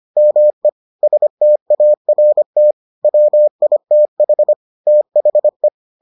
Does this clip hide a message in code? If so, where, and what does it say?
Yes: Morse code from 0.2 to 5.7 s, heard as a keyed tone, about 25 wpm, 600 Hz. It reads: ME START WITH THE